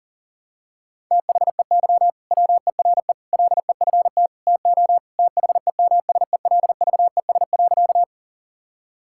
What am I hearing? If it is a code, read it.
Morse code, heard as a keyed tone, about 40 wpm, 700 Hz. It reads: THEY WERE LEFT TO THEMSELVES.
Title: Sentences from Top 1000 Words - Review Part 5 - 40wpm